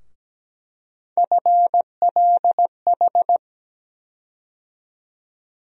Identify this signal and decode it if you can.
Morse code, heard as a keyed tone, about 17 wpm, 700 Hz. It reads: FLH
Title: USA Call Sign Postfixes - 3 Letters 17wpm